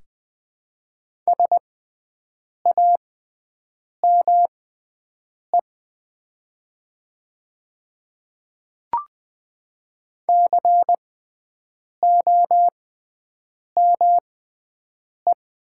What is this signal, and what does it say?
Morse code, heard as a keyed tone, about 20 wpm, spaced as for 7 wpm, 700 Hz. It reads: SAME COME
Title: 6x Character Spacing - Top 200 Words - 20wpm